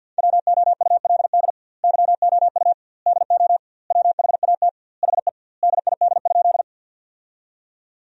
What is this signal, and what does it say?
Morse code, heard as a keyed tone, about 50 wpm, 700 Hz. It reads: WOULD YOU DO WHAT HE DID?
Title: 2x Word-Spacing - Sentences from Top 100 Words - 50wpm